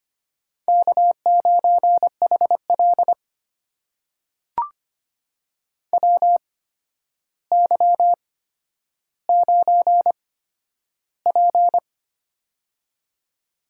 Morse code, heard as a keyed tone, about 25 wpm, spaced as for 11 wpm, 700 Hz. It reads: K9HL WY9P
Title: USA Call Signs - 4 Characters - 8x Character Spacing - 25wpm